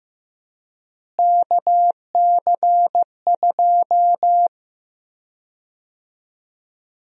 Morse code, heard as a keyed tone, about 15 wpm, 700 Hz. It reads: KC2
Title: USA Call Sign Prefixes - Repeated 2x - 15wpm